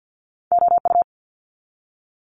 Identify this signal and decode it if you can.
Morse code, heard as a keyed tone, about 50 wpm, 700 Hz. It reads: OU
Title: Multi-letter Phonemes 50wpm